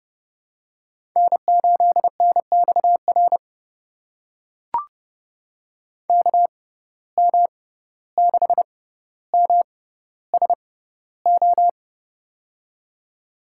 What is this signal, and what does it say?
Morse code, heard as a keyed tone, about 30 wpm, spaced as for 15 wpm, 700 Hz. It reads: N8NXR KM6MSO
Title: USA Call Signs - 6x Character Spacing - 30wpm